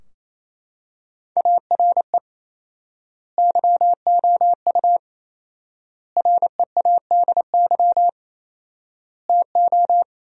Morse code, heard as a keyed tone, about 28 wpm, 700 Hz. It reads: ARE YOU READY TO
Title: 4x Word-Spacing - Sentences from Top 400 Words - 28wpm